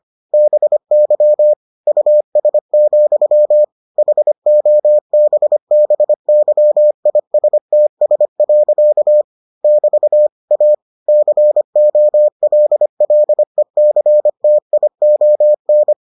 Morse code, heard as a keyed tone, about 25 wpm, 600 Hz. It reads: BY US, HOBBYISTS. <BT> A COLLECTION